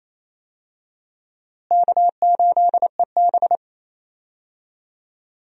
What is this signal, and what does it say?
Morse code, heard as a keyed tone, about 28 wpm, 700 Hz. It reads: K8EB
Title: USA Call Signs - 4 Characters - Repeated 2x - 28wpm